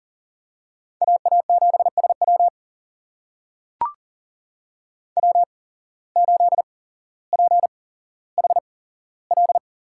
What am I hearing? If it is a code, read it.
Morse code, heard as a keyed tone, about 37 wpm, spaced as for 17 wpm, 700 Hz. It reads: AA7SW W8PHL
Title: USA Call Signs - 5 Characters - 8x Character Spacing - 40wpm